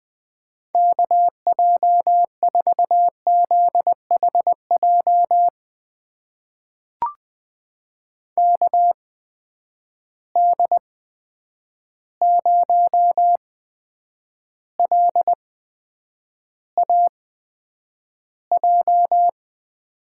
Morse code, heard as a keyed tone, about 20 wpm, spaced as for 9 wpm, 700 Hz. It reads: KJ4ZHJ KD0LAJ
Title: USA Call Signs - 6 Characters - 8x Character Spacing - 20wpm